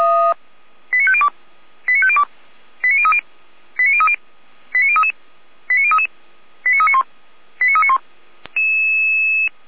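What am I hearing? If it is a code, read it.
Five-tone selective call (ZVEI), heard as quick runs of five tones, tones of about 70 ms. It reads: 89502 89502 89030 89030 89033 89033 89391 89391
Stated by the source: ZVEI five-tone selective call received on an SDR.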